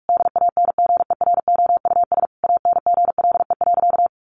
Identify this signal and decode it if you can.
Morse code, heard as a keyed tone, about 44 wpm, 700 Hz. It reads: DANGEROUS ANGLE.
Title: Sentences from Top 1500 Words 45wpm